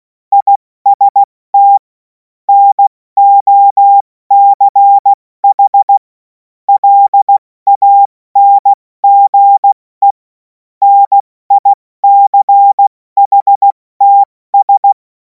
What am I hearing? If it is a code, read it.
Morse code, heard as a keyed tone, about 16 wpm, 800 Hz. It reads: IST NOCH LANGE NICHTS
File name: Morsekurs / Witze / Witz 11. August 2026 70 F80